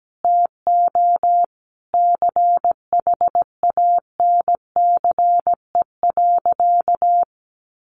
Morse code, heard as a keyed tone, about 17 wpm, 700 Hz. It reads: TO CHANCE.
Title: Sentences from Top 1400 Words 17wpm